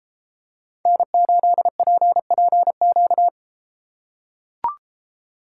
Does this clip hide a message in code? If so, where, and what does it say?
Yes: Morse code from 0.9 to 3.3 s, heard as a keyed tone, about 33 wpm, 700 Hz. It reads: N8PPQ